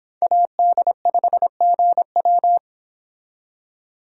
Morse code, heard as a keyed tone, about 29 wpm, 700 Hz. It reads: AD5GW